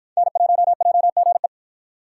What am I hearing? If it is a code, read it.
Morse code, heard as a keyed tone, about 53 wpm, 700 Hz. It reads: N1JGE